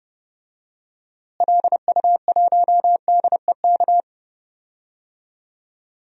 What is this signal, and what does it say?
Morse code, heard as a keyed tone, about 32 wpm, 700 Hz. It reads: LU1DEK
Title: DXCC-100 Call Signs - 6 Characters - Repeated 3x - 30wpm